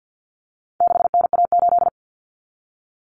Morse code, heard as a keyed tone, about 50 wpm, 700 Hz. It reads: 6NA8
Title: Common Cut Numbers - 4-Digit 50wpm